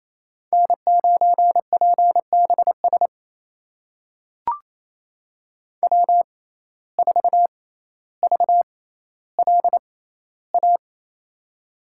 Morse code, heard as a keyed tone, about 28 wpm, spaced as for 14 wpm, 700 Hz. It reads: N9PBS W4VLA
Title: USA Call Signs - 5 Characters - 6x Character Spacing - 28wpm